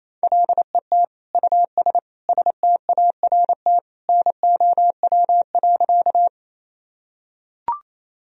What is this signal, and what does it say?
Morse code, heard as a keyed tone, about 28 wpm, 700 Hz. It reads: LET US START NOW.
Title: Sentences from Top 1000 Words - Review Part 1 - 28wpm